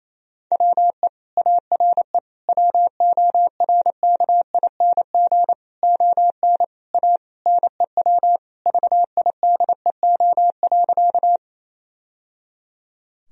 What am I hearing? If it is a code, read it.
Morse code, heard as a keyed tone, about 28 wpm, 700 Hz. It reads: WE ARE WORKING ON A NEW VIDEO.
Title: Sentences from Top 1000 Words - Review Part 4 - 28wpm